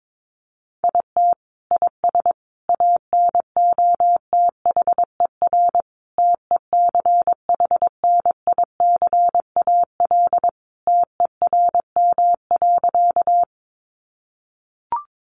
Morse code, heard as a keyed tone, about 22 wpm, 700 Hz. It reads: IT IS ANOTHER TECHNICAL TERM.
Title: Sentences from Top 1100 Words 22wpm